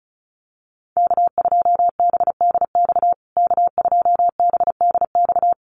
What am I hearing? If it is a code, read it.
Morse code, heard as a keyed tone, about 34 wpm, 700 Hz. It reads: K2BDX K2BDX
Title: USA Call Signs Repeated 2x 35wpm